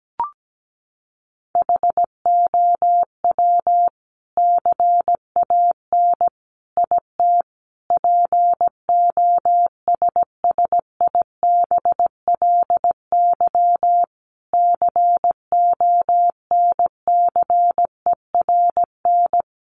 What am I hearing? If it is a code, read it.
Morse code, heard as a keyed tone, about 17 wpm, 700 Hz. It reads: HOW CAN IT POSSIBLY CONCERN
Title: Sentences from Top 1300 Words 17wpm